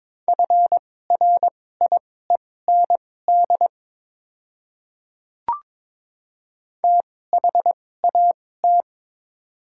Morse code, heard as a keyed tone, about 22 wpm, spaced as for 16 wpm, 700 Hz. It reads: FRIEND THAT
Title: 2x Character Spacing - Top 300 Words - 22wpm